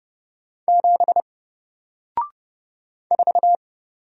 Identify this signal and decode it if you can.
Morse code, heard as a keyed tone, about 29 wpm, 700 Hz. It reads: 7 4
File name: Single-Letter-Number-30wpm